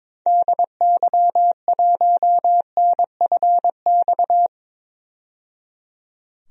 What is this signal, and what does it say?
Morse code, heard as a keyed tone, about 21 wpm, 700 Hz. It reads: DY1NFX
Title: DXCC-100 Call Signs - 6 Characters - 22wpm